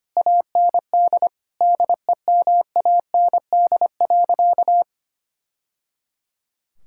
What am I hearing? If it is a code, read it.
Morse code, heard as a keyed tone, about 25 wpm, 700 Hz. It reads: AND DEMAND.
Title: Sentences from Top 1000 Words - Review Part 5 - 25wpm